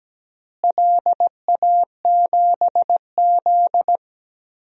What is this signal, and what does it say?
Morse code, heard as a keyed tone, about 17 wpm, 700 Hz. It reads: LA7Z
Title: DXCC-100 Call Signs - 4 Characters - Repeated 3x - 17wpm